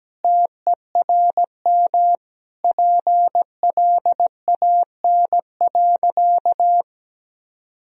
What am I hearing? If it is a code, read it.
Morse code, heard as a keyed tone, about 17 wpm, 700 Hz. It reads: TERM PLAN.
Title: Sentences from Top 1000 Words - Review Part 4 - 17wpm